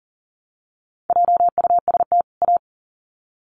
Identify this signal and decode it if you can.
Morse code, heard as a keyed tone, about 40 wpm, 700 Hz. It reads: JUST A
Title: Sets of 2 Words 40wpm